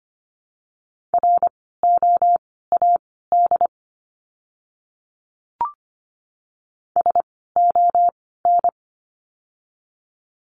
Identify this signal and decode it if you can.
Morse code, heard as a keyed tone, about 25 wpm, spaced as for 16 wpm, 700 Hz. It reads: ROAD SON
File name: course / 25:10wpm / 094 Lesson 16f - Letter c - Words 25wpm